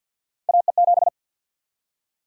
Morse code, heard as a keyed tone, about 49 wpm, 700 Hz. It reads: AE7
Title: USA Call Sign Prefixes - Repeated 2x - 50wpm